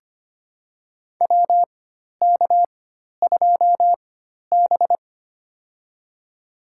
Morse code, heard as a keyed tone, about 25 wpm, spaced as for 12 wpm, 700 Hz. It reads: WK2B
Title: USA Call Signs - 4 Characters - 4x Character Spacing - 25wpm